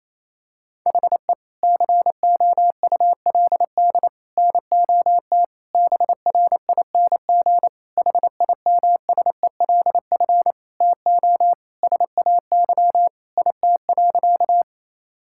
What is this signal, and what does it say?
Morse code, heard as a keyed tone, about 28 wpm, 700 Hz. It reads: HE COULD NOT BRING HIMSELF TO SAY IT.
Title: Sentences from Top 1600 Words 28wpm